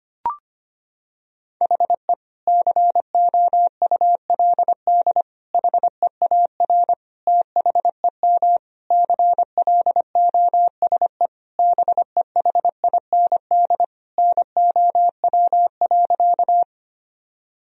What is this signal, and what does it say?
Morse code, heard as a keyed tone, about 25 wpm, 700 Hz. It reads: HE COULD HEAR THEM CLOSE BEHIND NOW.
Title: Sentences from Top 1000 Words - Review Part 1 - 25wpm